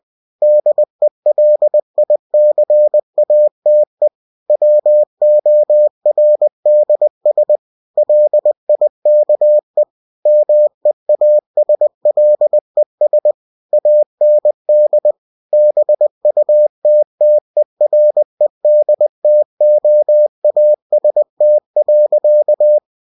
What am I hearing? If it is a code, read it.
Morse code, heard as a keyed tone, about 20 wpm, 600 Hz. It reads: DELICATE WORDS LIKE MEASLES AND BUTTERED TOAST.
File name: cw-books / winnie-the-pooh / winnie-20wpm-page-0029